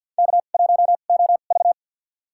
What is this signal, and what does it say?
Morse code, heard as a keyed tone, about 51 wpm, spaced as for 36 wpm, 700 Hz. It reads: K1OV
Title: USA Call Signs - 4 Characters - 2x Character Spacing - 50wpm